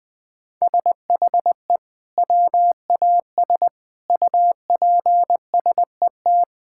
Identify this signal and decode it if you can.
Morse code, heard as a keyed tone, about 20 wpm, 700 Hz. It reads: SHE WAS UPSET
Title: Sentences from Top 1000 Words - Review Part 5 - 20wpm